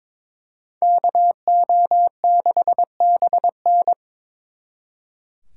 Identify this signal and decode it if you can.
Morse code, heard as a keyed tone, about 25 wpm, 700 Hz. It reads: KO6BN
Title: Speed Racing - USA 2x2 Call Signs 22wpm